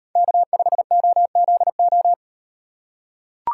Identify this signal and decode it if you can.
Morse code, heard as a keyed tone, about 38 wpm, 700 Hz. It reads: K5OZO